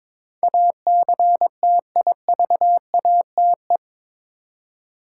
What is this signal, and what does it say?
Morse code, heard as a keyed tone, about 21 wpm, 700 Hz. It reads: ACTIVATE
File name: practice / 22wpm / Eight Letter Words - 22wpm